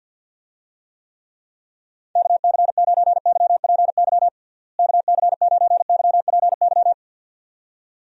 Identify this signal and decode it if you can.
Morse code, heard as a keyed tone, about 53 wpm, 700 Hz. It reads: KC9YPY KC9YPY
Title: USA Call Signs - 6 Characters - Repeated 2x - 50wpm